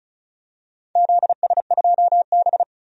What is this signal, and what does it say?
Morse code, heard as a keyed tone, about 34 wpm, 700 Hz. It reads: ZS2B
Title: DXCC-100 Call Signs - 4 Characters - 35wpm